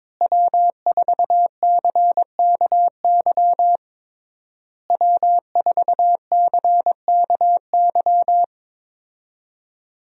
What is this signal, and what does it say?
Morse code, heard as a keyed tone, about 21 wpm, 700 Hz. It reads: W4CKY W4CKY
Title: USA Call Signs - 5 Characters - Repeated 3x - 22wpm